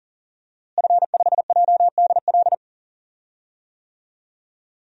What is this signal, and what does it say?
Morse code, heard as a keyed tone, about 38 wpm, 700 Hz. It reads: F5JDL
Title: DXCC-100 Call Signs - 5 Characters - Repeated 2x - 40wpm